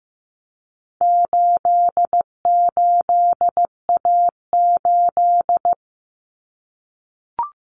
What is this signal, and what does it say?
Morse code, heard as a keyed tone, about 15 wpm, 700 Hz. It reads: 88A8
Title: Common Cut Numbers - 4-Digit 15wpm